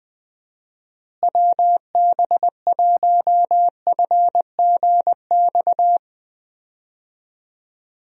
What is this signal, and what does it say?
Morse code, heard as a keyed tone, about 20 wpm, 700 Hz. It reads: WB1FGX
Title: USA Call Signs - 6 Characters - Repeated 3x - 20wpm